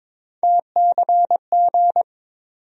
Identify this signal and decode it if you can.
Morse code, heard as a keyed tone, about 22 wpm, 700 Hz. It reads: TCG
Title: USA Call Sign Postfixes - 3 Letters 22wpm